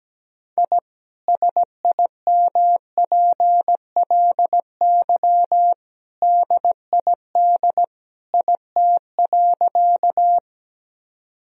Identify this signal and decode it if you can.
Morse code, heard as a keyed tone, about 17 wpm, 700 Hz. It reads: I SIMPLY DID IT.
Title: Sentences from Top 1000 Words - Review Part 4 - 17wpm